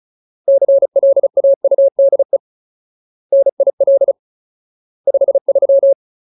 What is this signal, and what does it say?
Morse code, heard as a keyed tone, about 35 wpm, 550 Hz. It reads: CLAUDE NIL 53
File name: Ultimate Word List / 35WPM Ultimate